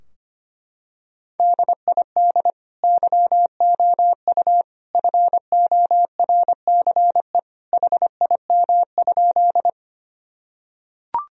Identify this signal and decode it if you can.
Morse code, heard as a keyed tone, about 25 wpm, 700 Hz. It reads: DID YOU FORCE HIM?
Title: Sentences from Top 1000 Words - Review Part 3 - 25wpm